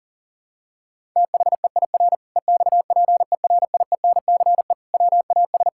Morse code, heard as a keyed tone, about 40 wpm, 700 Hz. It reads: THEIR EXPERIENCE WAS
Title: Sentences from Top 1000 Words - Review Part 5 - 40wpm